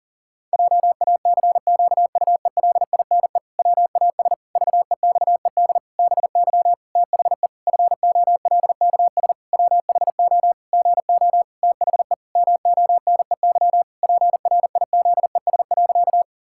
Morse code, heard as a keyed tone, about 37 wpm, 700 Hz. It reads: JACQUELINE WAS VEXED BY THE FOLKS WHO GO THE MONEY PRIZES.
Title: Pangram Challenge v2 - 40wpm